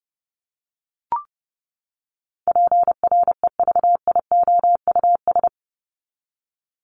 Morse code, heard as a keyed tone, about 30 wpm, 700 Hz. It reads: PREVIOUS
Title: LY Suffix - Top 100 Words with Base Words 30wpm